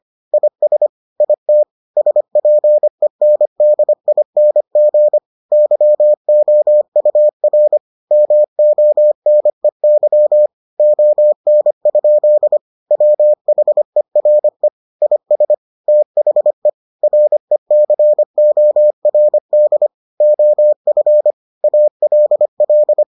Morse code, heard as a keyed tone, about 25 wpm, 600 Hz. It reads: IS IT SPENDING YOUR MONEY ON? WHERE IS THE RECORD OF ALL